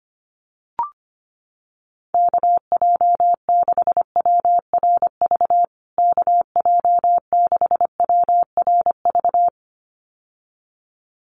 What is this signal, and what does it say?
Morse code, heard as a keyed tone, about 25 wpm, 700 Hz. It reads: KJ6WRV KJ6WRV
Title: USA Call Signs Repeated 2x 25wpm